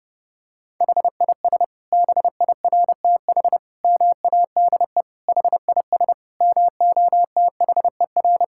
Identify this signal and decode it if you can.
Morse code, heard as a keyed tone, about 30 wpm, 700 Hz. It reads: HIS BIRTH MADE HIS MOTHER
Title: Sentences from Top 1000 Words - Review Part 5 - 30wpm